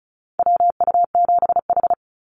Morse code, heard as a keyed tone, about 34 wpm, 700 Hz. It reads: WU7H
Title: NA SOTA Activator Callsigns 35wpm